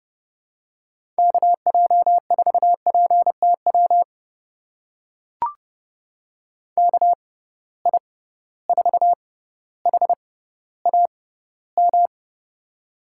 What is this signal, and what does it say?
Morse code, heard as a keyed tone, about 32 wpm, spaced as for 16 wpm, 700 Hz. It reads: KJ4PTW KI4HAM
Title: USA Call Signs - 6x Character Spacing - 30wpm